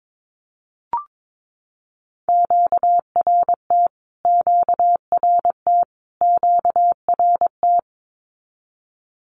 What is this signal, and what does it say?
Morse code, heard as a keyed tone, about 22 wpm, 700 Hz. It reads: QRT QRT QRT
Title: Q-Codes Repeated 3x 22wpm